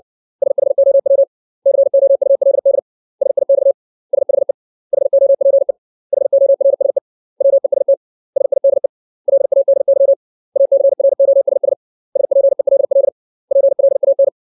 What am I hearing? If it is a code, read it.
Morse code, heard as a keyed tone, about 60 wpm, 550 Hz. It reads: SHOW COULD SIX SHE HOPE HORSE WEST SIDE LADY ACROSS SPELL PLAN
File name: 632 words / 632 Words 60WPM